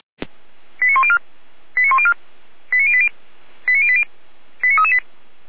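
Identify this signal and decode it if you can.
Five-tone selective call (ZVEI), heard as quick runs of five tones, tones of about 70 ms. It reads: 89104 89104 89080 89080 89338
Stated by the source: Five-tone ZVEI selective call 89020 received on an SDR.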